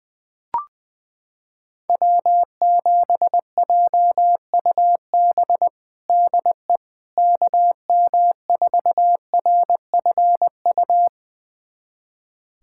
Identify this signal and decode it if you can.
Morse code, heard as a keyed tone, about 21 wpm, 700 Hz. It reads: W7JUB DE KM4RFU
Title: USA Callsign DE Callsign 20wpm